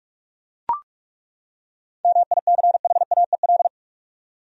Morse code, heard as a keyed tone, about 41 wpm, 700 Hz. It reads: MICHAEL
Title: USA Common Names - 45wpm